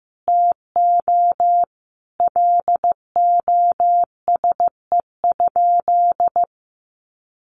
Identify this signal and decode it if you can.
Morse code, heard as a keyed tone, about 15 wpm, 700 Hz. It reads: TO LOSE?
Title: Sentences from Top 1100 Words 15wpm